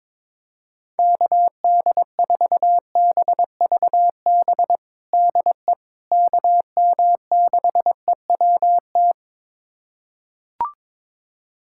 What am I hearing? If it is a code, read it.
Morse code, heard as a keyed tone, about 22 wpm, 700 Hz. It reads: KD4BVB DE KM6EWT
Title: USA Callsign DE Callsign - 6 Characters - 22wpm